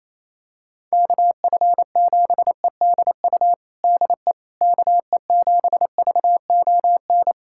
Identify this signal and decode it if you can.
Morse code, heard as a keyed tone, about 28 wpm, 700 Hz. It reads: KF7EDU DE KE7VON
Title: USA Callsign DE Callsign - 6 Characters - 28wpm